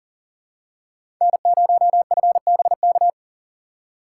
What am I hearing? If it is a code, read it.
Morse code, heard as a keyed tone, about 40 wpm, 700 Hz. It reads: N0FBK
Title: USA Call Signs - 5 Characters - 2x Character Spacing - 40wpm